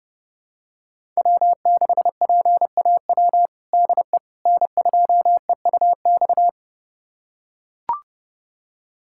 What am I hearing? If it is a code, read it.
Morse code, heard as a keyed tone, about 30 wpm, 700 Hz. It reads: W6PAW DE N2EUX
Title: USA Callsign DE Callsign - 5 Characters - 30wpm